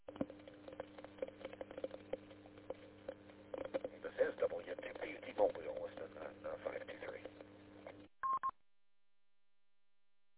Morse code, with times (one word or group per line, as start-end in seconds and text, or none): none